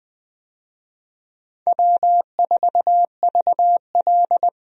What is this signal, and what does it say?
Morse code, heard as a keyed tone, about 20 wpm, 700 Hz. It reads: W4VL
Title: USA Call Signs - 4 Characters - 20wpm